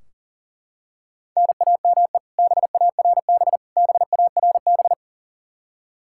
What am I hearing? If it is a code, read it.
Morse code, heard as a keyed tone, about 40 wpm, 700 Hz. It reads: NAME BARB BARB
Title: Practice QSO - Exchange 1 - Beginning - 40wpm